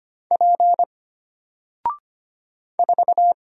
Morse code, heard as a keyed tone, about 25 wpm, 700 Hz. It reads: P 4
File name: Single-Letter-Number-25wpm